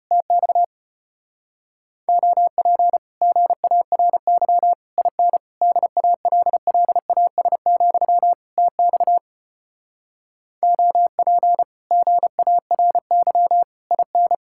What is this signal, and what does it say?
Morse code, heard as a keyed tone, about 33 wpm, 700 Hz. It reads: TX OP GARY IN DALLAS, TX OP GARY IN